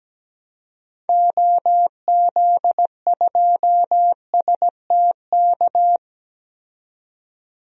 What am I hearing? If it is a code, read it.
Morse code, heard as a keyed tone, about 17 wpm, 700 Hz. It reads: OZ2STK
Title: DXCC-100 Call Signs - 6 Characters - 17wpm